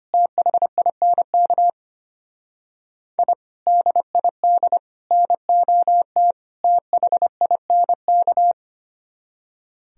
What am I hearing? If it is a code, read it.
Morse code, heard as a keyed tone, about 31 wpm, 700 Hz. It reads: THINK I DID NOT THINK
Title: Speed-Racing - Sets of 4 Words 25wpm